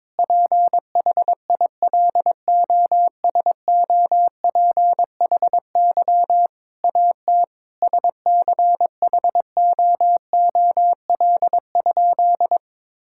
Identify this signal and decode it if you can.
Morse code, heard as a keyed tone, about 22 wpm, 700 Hz. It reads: PHILOSOPHY AT SCHOOL?
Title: Sentences from Top 1000 Words - Review Part 6 - 22wpm